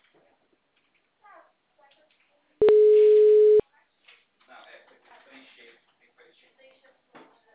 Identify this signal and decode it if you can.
Ringback tone (Europe, 425 Hz), with speech.